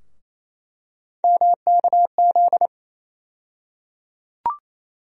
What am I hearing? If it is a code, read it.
Morse code, heard as a keyed tone, about 28 wpm, 700 Hz. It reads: MKZ
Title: USA Call Sign Postfixes - 3 Letters 28wpm